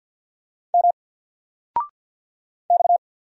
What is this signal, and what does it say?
Morse code, heard as a keyed tone, about 48 wpm, 700 Hz. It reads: M X
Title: Single Character (Letter:Number:Punctuation:Prosign) Alt 50wpm